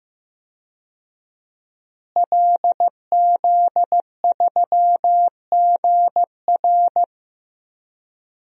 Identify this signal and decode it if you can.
Morse code, heard as a keyed tone, about 15 wpm, 700 Hz. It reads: LZ3GR
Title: DXCC-100 Call Signs - 5 Characters - Repeated 3x - 15wpm